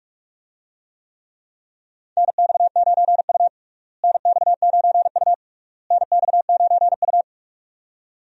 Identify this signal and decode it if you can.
Morse code, heard as a keyed tone, about 46 wpm, 700 Hz. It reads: NX9U NX9U NX9U
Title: USA Call Signs - 4 Characters - Repeated 3x - 45wpm